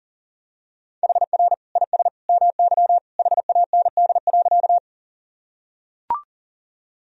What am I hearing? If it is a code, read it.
Morse code, heard as a keyed tone, about 37 wpm, 700 Hz. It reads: HR IS MY HAND.
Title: Sentences from Top 200 Words and Abbreviations 40wpm